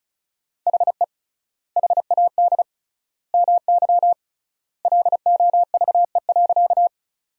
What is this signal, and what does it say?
Morse code, heard as a keyed tone, about 34 wpm, 700 Hz. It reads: HE HAD MY LOVE.
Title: 3x Word-Spacing - Sentences from Top 400 Words - 35wpm